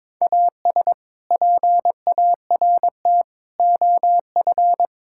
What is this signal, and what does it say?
Morse code, heard as a keyed tone, about 22 wpm, 700 Hz. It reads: AS PART OF
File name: practice / 22wpm / Sentences with 3-Word Phrases - 22wpm